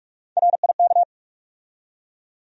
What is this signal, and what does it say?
Morse code, heard as a keyed tone, about 43 wpm, 700 Hz. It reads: RIK
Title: USA Call Sign Postfixes - 3 Letters 45wpm